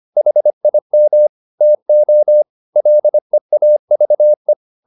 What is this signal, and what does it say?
Morse code, heard as a keyed tone, about 25 wpm, 600 Hz. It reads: HIM TO LEAVE
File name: the old man and the sea part 1 25wpm